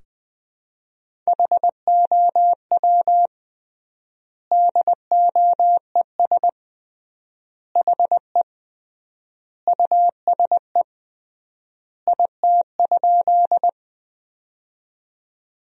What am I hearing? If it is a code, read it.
Morse code, heard as a keyed tone, about 21 wpm, 700 Hz. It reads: HOW DOES HE USE IT?
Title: 3x Word-Spacing - Sentences from Top 200 Words - 20wpm